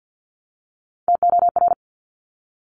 Morse code, heard as a keyed tone, about 50 wpm, 700 Hz. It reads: TOR
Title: Top Three Letter Combinations in Words 50wpm